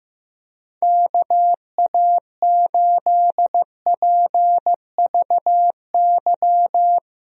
Morse code, heard as a keyed tone, about 15 wpm, 700 Hz. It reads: KA8PVY
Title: USA Call Signs - 6 Characters - Repeated 3x - 15wpm